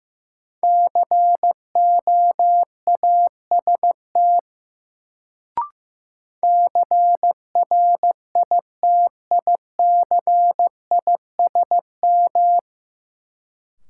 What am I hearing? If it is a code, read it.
Morse code, heard as a keyed tone, about 16 wpm, 700 Hz. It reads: COAST CRITICISM
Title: Sentences from Top 1300 Words - New Word Drill Alt - 15wpm